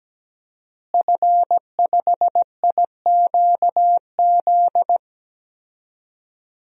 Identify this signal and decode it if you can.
Morse code, heard as a keyed tone, about 17 wpm, 700 Hz. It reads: F5IQZ
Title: DXCC-100 Call Signs - 5 Characters - Repeated 2x - 17wpm